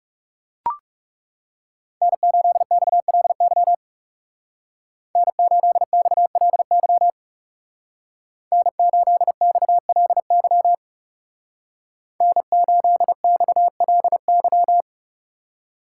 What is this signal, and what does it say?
Morse code, heard as a keyed tone, about 36 wpm, 700 Hz. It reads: N8XLY N8XLY N8XLY N8XLY
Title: Speed Racing - USA 1x3 Call Signs 30wpm